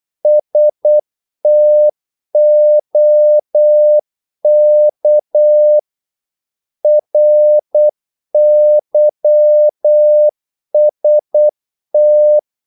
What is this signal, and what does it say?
Morse code, heard as a keyed tone, about 8 wpm, 600 Hz. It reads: STOK RYST
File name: Korte ord DK 8wpm